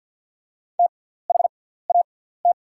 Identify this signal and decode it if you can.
Morse code, heard as a keyed tone, about 48 wpm, spaced as for 17 wpm, 700 Hz. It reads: THAT
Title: 6x Character Spacing - Top 500 Words - 50wpm